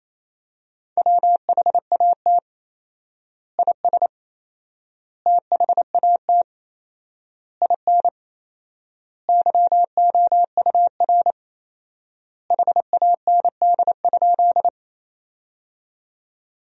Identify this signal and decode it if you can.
Morse code, heard as a keyed tone, about 27 wpm, 700 Hz. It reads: WHAT IS THAT IN YOUR HAND?
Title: 4x Word-Spacing - Sentences from Top 200 Words - 28wpm